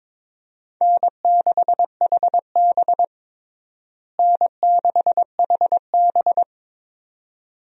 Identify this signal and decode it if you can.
Morse code, heard as a keyed tone, about 22 wpm, 700 Hz. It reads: N6HB N6HB
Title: USA Call Signs - 4 Characters - Repeated 2x - 22wpm